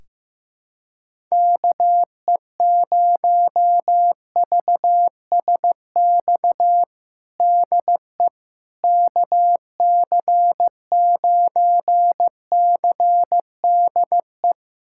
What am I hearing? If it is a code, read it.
Morse code, heard as a keyed tone, about 15 wpm, 700 Hz. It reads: KE0VSX DE KC9CDE
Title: USA Callsign DE Callsign 15wpm